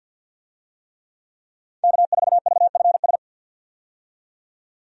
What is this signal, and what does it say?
Morse code, heard as a keyed tone, about 52 wpm, 700 Hz. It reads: K4VFS
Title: USA Call Signs - 5 Characters - 50wpm